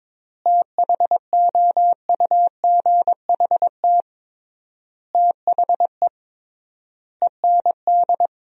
Morse code, heard as a keyed tone, about 22 wpm, 700 Hz. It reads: THOUGHT THE END
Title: 3x Word-Spacing - Sentences from Top 300 Words - 22wpm